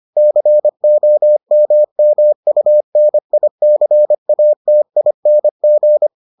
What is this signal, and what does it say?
Morse code, heard as a keyed tone, about 25 wpm, 600 Hz. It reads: COMMUNICATING